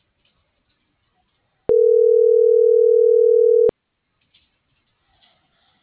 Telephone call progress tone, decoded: ringback tone

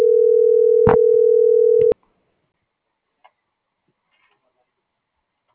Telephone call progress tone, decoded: ringback tone

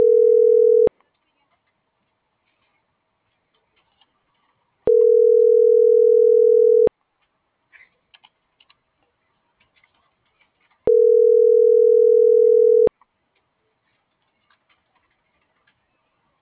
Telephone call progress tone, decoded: ringback tone

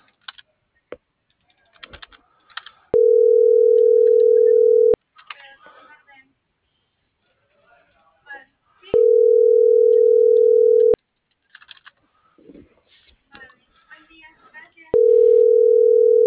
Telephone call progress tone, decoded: ringback tone

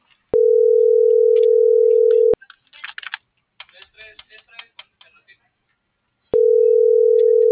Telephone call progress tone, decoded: ringback tone

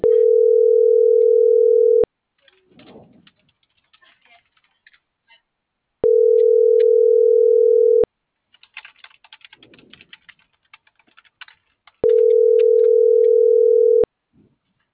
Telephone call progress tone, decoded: ringback tone

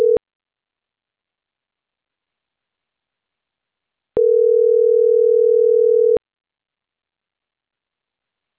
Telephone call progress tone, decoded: ringback tone

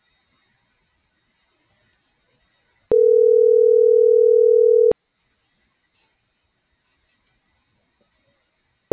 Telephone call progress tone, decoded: ringback tone

